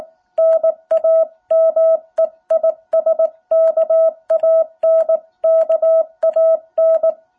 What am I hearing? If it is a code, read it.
Morse code, heard as a keyed tone, about 19 wpm, 660 Hz. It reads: NAME IS KAN KAN
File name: jh8cbh / 2021cw